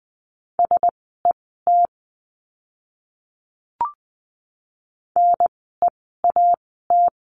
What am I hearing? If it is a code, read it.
Morse code, heard as a keyed tone, about 20 wpm, spaced as for 14 wpm, 700 Hz. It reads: SET NEAT
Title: 054 Lesson 10f - Letter r - Words 20wpm